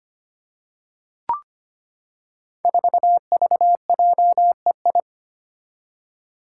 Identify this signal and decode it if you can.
Morse code, heard as a keyed tone, about 25 wpm, 700 Hz. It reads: 4VJEI
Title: Five Challenging Letter-Number Combinations 25wpm